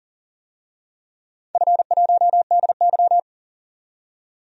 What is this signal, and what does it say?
Morse code, heard as a keyed tone, about 40 wpm, 700 Hz. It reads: F1DY
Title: DXCC-100 Call Signs - 4 Characters - 40wpm